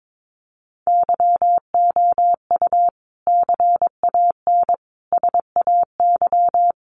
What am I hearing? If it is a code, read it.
Morse code, heard as a keyed tone, about 22 wpm, 700 Hz. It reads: YOU CAN SAY
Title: Sentences from Top 200 Words 22wpm